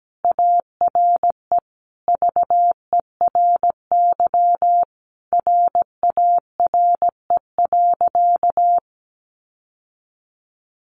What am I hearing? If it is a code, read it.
Morse code, heard as a keyed tone, about 17 wpm, 700 Hz. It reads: ARE VERY RARE.
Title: Sentences from Top 1100 Words 17wpm